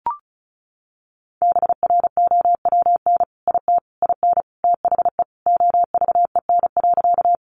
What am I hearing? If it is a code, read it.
Morse code, heard as a keyed tone, about 34 wpm, 700 Hz. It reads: BROWN IT IN THE OVEN.
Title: Sentences from Top 1200 Words 35wpm